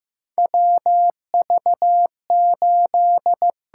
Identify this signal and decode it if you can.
Morse code, heard as a keyed tone, about 15 wpm, 700 Hz. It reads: WV8
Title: USA Call Signs - 4 Characters - 6x Character Spacing - 15wpm